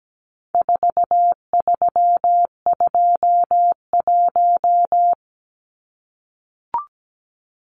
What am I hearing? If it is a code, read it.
Morse code, heard as a keyed tone, about 17 wpm, 700 Hz. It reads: 4321
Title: Numbers - 4-Digit 17wpm